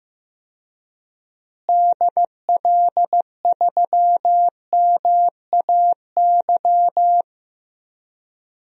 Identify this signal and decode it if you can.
Morse code, heard as a keyed tone, about 15 wpm, 700 Hz. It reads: DL3MAY
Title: DXCC-100 Call Signs - 6 Characters - Repeated 3x - 15wpm